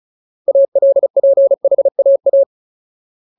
Morse code, heard as a keyed tone, about 35 wpm, 550 Hz. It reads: ALPHAA
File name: Ultimate Word List / 35WPM Ultimate